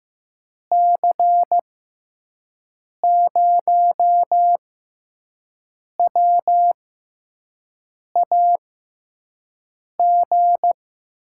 Morse code, heard as a keyed tone, about 15 wpm, spaced as for 5 wpm, 700 Hz. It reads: C0WAG
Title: USA Call Signs - 6x Character Spacing - 15wpm